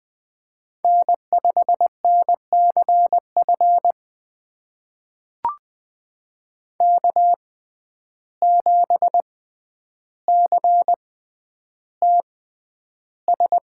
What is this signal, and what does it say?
Morse code, heard as a keyed tone, about 20 wpm, spaced as for 10 wpm, 700 Hz. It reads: N5NCF K7CTS
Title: USA Call Signs - 5 Characters - 6x Character Spacing - 20wpm